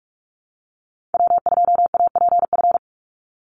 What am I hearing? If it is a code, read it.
Morse code, heard as a keyed tone, about 44 wpm, 700 Hz. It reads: W2APF
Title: Call Signs Repeated 2x - 45wpm